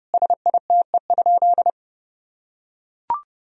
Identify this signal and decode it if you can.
Morse code, heard as a keyed tone, about 30 wpm, 700 Hz. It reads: SITE?
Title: Sentences from Top 1000 Words - Review Part 4 - 30wpm